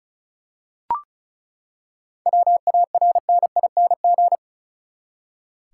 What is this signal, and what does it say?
Morse code, heard as a keyed tone, about 34 wpm, 700 Hz. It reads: WARNING
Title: Seven Letter Words - 35wpm